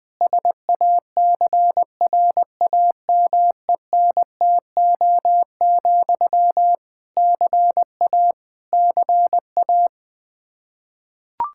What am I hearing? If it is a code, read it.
Morse code, heard as a keyed tone, about 21 wpm, 700 Hz. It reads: SACRAMENTO, CA CA
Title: Practice QSO - Exchange 1 - Beginning - 20wpm